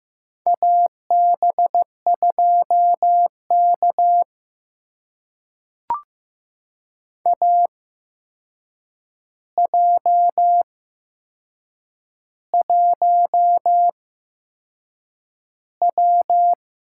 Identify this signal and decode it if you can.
Morse code, heard as a keyed tone, about 15 wpm, spaced as for 6 wpm, 700 Hz. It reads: AB2K AJ1W
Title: USA Call Signs - 4 Characters - 8x Character Spacing - 15wpm